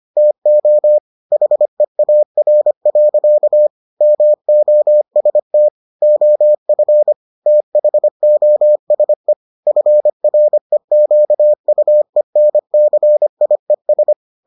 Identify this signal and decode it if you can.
Morse code, heard as a keyed tone, about 25 wpm, 600 Hz. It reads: TO HEAR. MOST OF THOSE FREQUENCIES